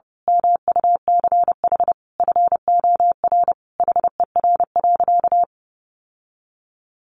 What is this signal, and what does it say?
Morse code, heard as a keyed tone, about 30 wpm, 700 Hz. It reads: MUCH FOR HER.
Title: Sentences from Top 1200 Words 30wpm